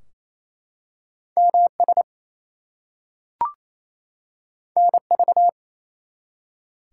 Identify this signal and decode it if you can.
Morse code, heard as a keyed tone, about 28 wpm, 700 Hz. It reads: MS NV
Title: US States and Canadian Provinces 28wpm